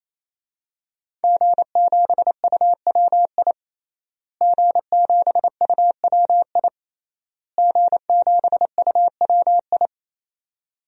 Morse code, heard as a keyed tone, about 27 wpm, 700 Hz. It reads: G7UWI G7UWI G7UWI
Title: DXCC-100 Call Signs - 5 Characters - Repeated 3x - 28wpm